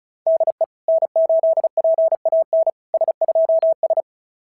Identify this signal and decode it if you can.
Morse code, heard as a keyed tone, about 36 wpm, 690 Hz. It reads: DE N8PAN S2S
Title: SOTA QSO Practice 35wpm